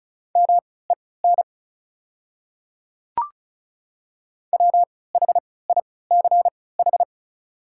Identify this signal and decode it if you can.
Morse code, heard as a keyed tone, about 36 wpm, spaced as for 20 wpm, 690 Hz. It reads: MEN WHICH
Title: 3x Character Spacing - Top 200 Words - 35wpm